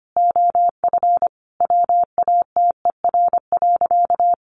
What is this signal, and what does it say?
Morse code, heard as a keyed tone, about 25 wpm, 700 Hz. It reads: OF WATER.
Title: Sentences from Top 700 Words 25wpm